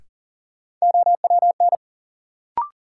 Morse code, heard as a keyed tone, about 40 wpm, 700 Hz. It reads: OWN
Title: Sentences from Top 300 Words - New Word Drill - 40wpm